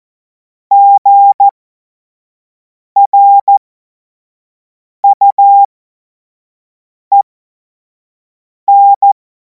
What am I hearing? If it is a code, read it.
Morse code, heard as a keyed tone, about 14 wpm, spaced as for 5 wpm, 800 Hz. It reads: GRUEN
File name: Morsekurs / Witze / Witz 07. Juli 2026 25 F70